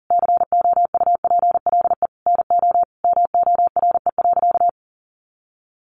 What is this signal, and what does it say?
Morse code, heard as a keyed tone, about 40 wpm, 700 Hz. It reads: COUPLE NO MORE.
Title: Sentences from Top 1100 Words 40wpm